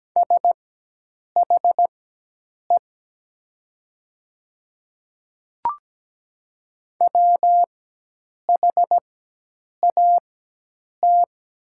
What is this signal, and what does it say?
Morse code, heard as a keyed tone, about 17 wpm, spaced as for 8 wpm, 700 Hz. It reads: SHE WHAT
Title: 4x Character Spacing - Top 100 Words - 17wpm